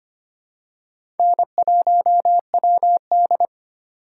Morse code, heard as a keyed tone, about 25 wpm, 700 Hz. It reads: N1WD
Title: USA Call Signs - 4x Character Spacing - 25wpm